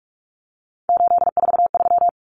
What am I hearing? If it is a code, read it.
Morse code, heard as a keyed tone, about 44 wpm, 700 Hz. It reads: 843
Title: Common Cut Numbers - 3-Digit 45wpm